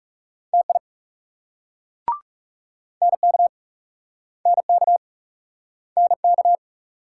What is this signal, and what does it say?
Morse code, heard as a keyed tone, about 41 wpm, 700 Hz. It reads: TI NK NK NK